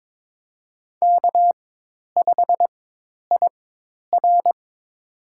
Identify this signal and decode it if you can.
Morse code, heard as a keyed tone, about 22 wpm, spaced as for 10 wpm, 700 Hz. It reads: K5IR
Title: USA Call Signs - 4 Characters - 4x Character Spacing - 22wpm